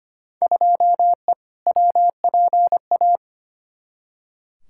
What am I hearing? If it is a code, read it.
Morse code, heard as a keyed tone, about 25 wpm, 700 Hz. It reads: 2E WPA
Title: Field Day Exchanges Encore - 25wpm